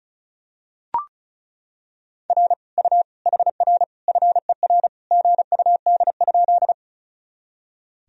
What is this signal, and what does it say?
Morse code, heard as a keyed tone, about 36 wpm, 700 Hz. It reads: R U HR FER GUD?
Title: Sentences from Top 200 Words and Abbreviations 35wpm